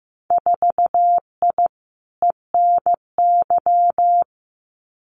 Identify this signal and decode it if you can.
Morse code, heard as a keyed tone, about 15 wpm, 700 Hz. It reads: 4I ENY
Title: Winter Field Day Exchanges Encore 15wpm